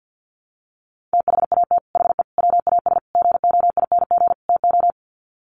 Morse code, heard as a keyed tone, about 48 wpm, 700 Hz. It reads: THAT HE WAS GOING TO
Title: Sets of 5 Words 50wpm